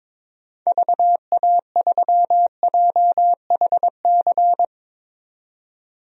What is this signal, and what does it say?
Morse code, heard as a keyed tone, about 22 wpm, 700 Hz. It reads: VA3JHC
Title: DXCC-100 Call Signs - 6 Characters - 22wpm